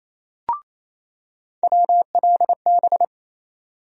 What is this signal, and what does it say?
Morse code, heard as a keyed tone, about 28 wpm, 700 Hz. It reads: WLB
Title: USA Call Sign Postfixes - 3 Letters 28wpm